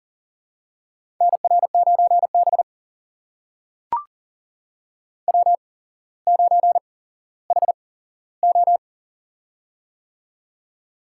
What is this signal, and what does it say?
Morse code, heard as a keyed tone, about 42 wpm, spaced as for 17 wpm, 700 Hz. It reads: NR9B W9HO
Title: USA Call Signs - 4 Characters - 8x Character Spacing - 40wpm